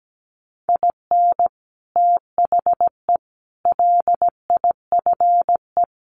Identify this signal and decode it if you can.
Morse code, heard as a keyed tone, about 17 wpm, 700 Hz. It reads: IN THE LIFE
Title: Sets of 5 Words Encore2 17wpm